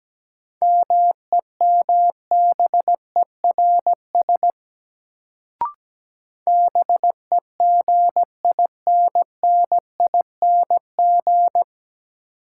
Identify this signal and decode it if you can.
Morse code, heard as a keyed tone, about 17 wpm, 700 Hz. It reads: MEMBERS BEGINNING
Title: Sets of 3 Words - New Word Drill - 17wpm